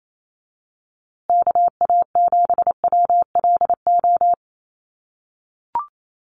Morse code, heard as a keyed tone, about 28 wpm, 700 Hz. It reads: KA7WLO
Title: USA Call Signs Repeated 2x 28wpm